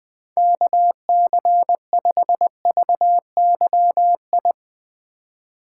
Morse code, heard as a keyed tone, about 20 wpm, 700 Hz. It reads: KC5VYI
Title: USA Call Signs - 6 Characters - 2x Character Spacing - 20wpm